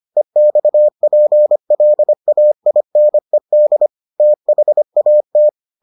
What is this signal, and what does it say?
Morse code, heard as a keyed tone, about 25 wpm, 600 Hz. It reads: EXPLAINED THAT